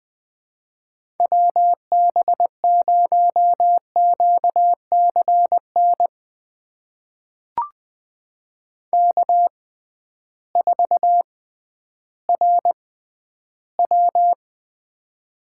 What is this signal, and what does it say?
Morse code, heard as a keyed tone, about 20 wpm, spaced as for 12 wpm, 700 Hz. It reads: WB0QCN K4RW
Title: USA Call Signs - 6x Character Spacing - 20wpm